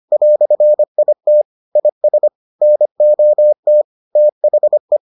Morse code, heard as a keyed tone, about 25 wpm, 600 Hz. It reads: "IT IS NOT THE